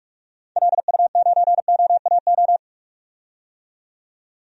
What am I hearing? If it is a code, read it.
Morse code, heard as a keyed tone, about 47 wpm, 700 Hz. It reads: LU9OAO
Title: DXCC-100 Call Signs - 6 Characters - 45wpm